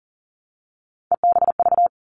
Morse code, heard as a keyed tone, about 40 wpm, 700 Hz. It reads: EBV
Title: All Cut Numbers - 3-Digit 40wpm